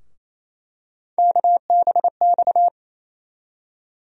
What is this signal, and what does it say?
Morse code, heard as a keyed tone, about 28 wpm, 700 Hz. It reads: KBX
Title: USA Call Sign Postfixes - 3 Letters 28wpm